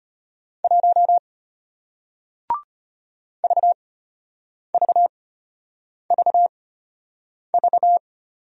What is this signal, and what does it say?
Morse code, heard as a keyed tone, about 34 wpm, spaced as for 8 wpm, 700 Hz. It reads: 1 VVVV